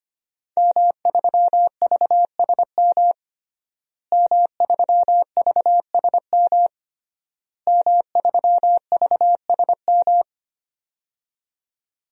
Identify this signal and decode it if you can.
Morse code, heard as a keyed tone, about 25 wpm, 700 Hz. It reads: M3VSM M3VSM M3VSM
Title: DXCC-100 Call Signs - 5 Characters - Repeated 3x - 25wpm